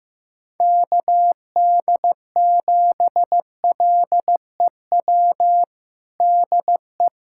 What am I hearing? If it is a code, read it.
Morse code, heard as a keyed tone, about 16 wpm, 700 Hz. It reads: KD7LEW DE
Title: USA Callsign DE Callsign 15wpm